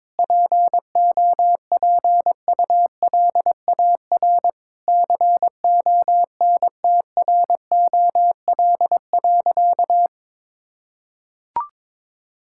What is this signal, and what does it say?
Morse code, heard as a keyed tone, about 22 wpm, 700 Hz. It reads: POPULAR CONTROL.
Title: Sentences from Top 1000 Words - Review Part 5 - 22wpm